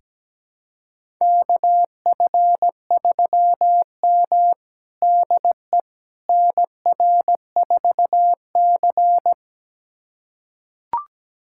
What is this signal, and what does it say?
Morse code, heard as a keyed tone, about 17 wpm, 700 Hz. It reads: KF3M DE NR4C
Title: USA Callsign DE Callsign - 4 Characters - 17wpm